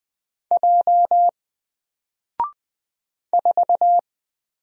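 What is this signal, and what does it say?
Morse code, heard as a keyed tone, about 20 wpm, 700 Hz. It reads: J 4
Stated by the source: Single letter and number Morse drill at 20 wpm, from N4LCD.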